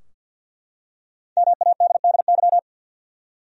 Morse code, heard as a keyed tone, about 52 wpm, 700 Hz. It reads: MADDY
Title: USA Common Names - 50wpm